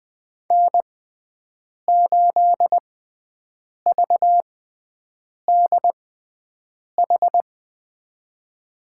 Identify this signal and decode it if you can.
Morse code, heard as a keyed tone, about 21 wpm, spaced as for 7 wpm, 700 Hz. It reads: N8VDH
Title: USA Call Signs - 6x Character Spacing - 20wpm